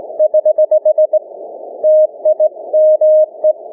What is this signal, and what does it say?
Morse code, heard as a keyed tone, about 20 wpm, 620 Hz. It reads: <HH> TIME
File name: cw / ssn 30M